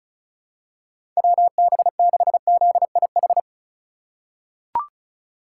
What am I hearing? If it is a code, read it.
Morse code, heard as a keyed tone, about 36 wpm, 700 Hz. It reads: WB6ZIH
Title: USA Call Signs - 6 Characters - 2x Character Spacing - 35wpm